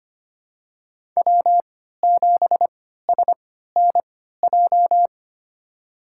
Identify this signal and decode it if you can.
Morse code, heard as a keyed tone, about 25 wpm, spaced as for 14 wpm, 700 Hz. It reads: W7SNJ
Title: USA Call Signs - 5 Characters - 3x Character Spacing - 25wpm